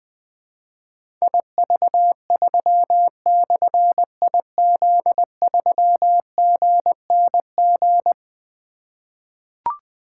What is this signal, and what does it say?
Morse code, heard as a keyed tone, about 20 wpm, 700 Hz. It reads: IV3/IZ3GNG
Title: Difficult Call Signs 20wpm